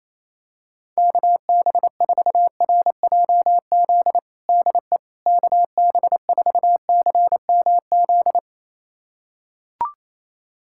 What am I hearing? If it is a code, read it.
Morse code, heard as a keyed tone, about 28 wpm, 700 Hz. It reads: KB4RJZ DE KB4CMZ
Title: USA Callsign DE Callsign - 6 Characters - 28wpm